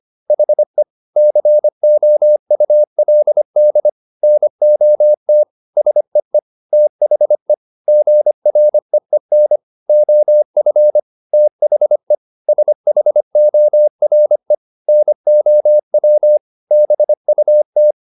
Morse code, heard as a keyed tone, about 25 wpm, 600 Hz. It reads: HE COULD NOT SEE THE GREEN OF THE SHORE NOW BUT
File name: the old man and the sea part 1 25wpm